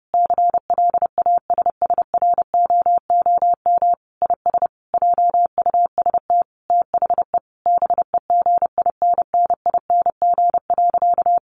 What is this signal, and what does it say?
Morse code, heard as a keyed tone, about 30 wpm, 700 Hz. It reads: CLASSROOM IS JUST THE BEGINNING.
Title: Sentences from Top 1500 Words 30wpm